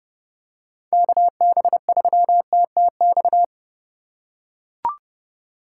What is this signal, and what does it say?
Morse code, heard as a keyed tone, about 32 wpm, 700 Hz. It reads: KB3TTX
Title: USA Call Signs - 6 Characters - 2x Character Spacing - 30wpm